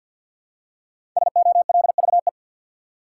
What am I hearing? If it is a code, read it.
Morse code, heard as a keyed tone, about 53 wpm, 700 Hz. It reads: SOLVE
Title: Five Letter Words - 50wpm